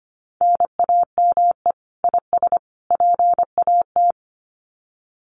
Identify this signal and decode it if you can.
Morse code, heard as a keyed tone, about 25 wpm, 700 Hz. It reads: NAME IS PAT
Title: Common QSO Phrases 25wpm